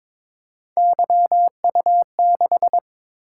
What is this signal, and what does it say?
Morse code, heard as a keyed tone, about 22 wpm, 700 Hz. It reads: YU6
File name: practice / 22wpm / Three Letter-Number Combinations - 22wpm